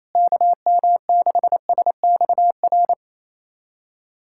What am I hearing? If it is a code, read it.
Morse code, heard as a keyed tone, about 28 wpm, 700 Hz. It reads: KM6SXR
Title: USA Callsign DE Callsign - 6 Characters - 28wpm